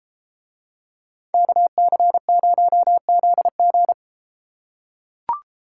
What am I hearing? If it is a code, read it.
Morse code, heard as a keyed tone, about 31 wpm, 700 Hz. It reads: KC0ZG